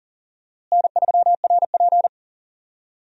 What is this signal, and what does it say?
Morse code, heard as a keyed tone, about 40 wpm, 700 Hz. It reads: N3RP
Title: USA Call Signs - 4 Characters - Repeated 2x - 40wpm